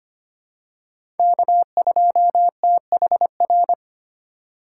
Morse code, heard as a keyed tone, about 25 wpm, 700 Hz. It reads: K2THR
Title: USA Call Signs - 5 Characters - 3x Character Spacing - 25wpm